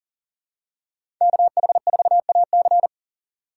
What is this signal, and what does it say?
Morse code, heard as a keyed tone, about 42 wpm, 700 Hz. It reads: KH4AC
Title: USA Call Signs - 5 Characters - 6x Character Spacing - 40wpm